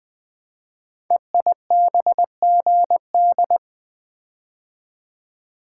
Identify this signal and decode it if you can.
Morse code, heard as a keyed tone, about 20 wpm, 700 Hz. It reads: EIBGD
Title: Code Groups - 5 Letters - 20wpm